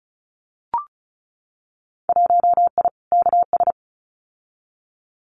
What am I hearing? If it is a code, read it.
Morse code, heard as a keyed tone, about 36 wpm, 700 Hz. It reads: 1I KS
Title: Winter Field Day Exchanges Encore 35wpm